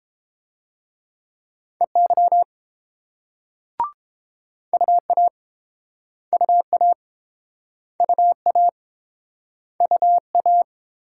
Morse code, heard as a keyed tone, about 30 wpm, 700 Hz. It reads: EY UA UA UA UA